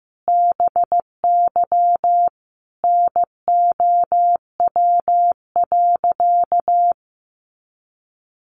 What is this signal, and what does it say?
Morse code, heard as a keyed tone, about 15 wpm, 700 Hz. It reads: BY NOW.
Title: Sentences from Top 1400 Words 15wpm